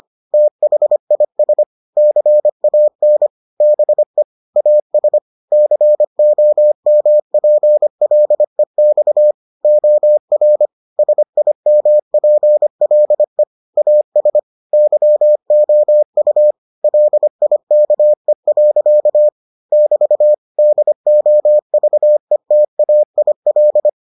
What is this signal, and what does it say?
Morse code, heard as a keyed tone, about 25 wpm, 600 Hz. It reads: THIS CAN BE AS COMPLEX OR SIMPLE AS YOU LIKE. <BT> DOVETAIL